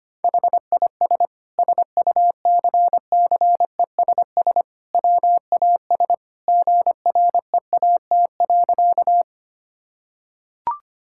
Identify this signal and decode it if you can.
Morse code, heard as a keyed tone, about 25 wpm, 700 Hz. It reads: HIS SUCCESS WAS GREAT.
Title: Sentences from Top 1000 Words - Review Part 6 - 25wpm